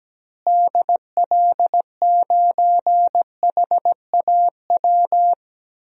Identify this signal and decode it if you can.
Morse code, heard as a keyed tone, about 17 wpm, 700 Hz. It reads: DL9HAW
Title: DXCC-100 Call Signs - 6 Characters - Repeated 3x - 17wpm